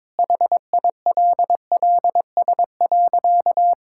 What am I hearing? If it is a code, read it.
Morse code, heard as a keyed tone, about 22 wpm, 700 Hz. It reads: HILLS.
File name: practice / 22wpm / Sentences with 2-Word Phrases - 22wpm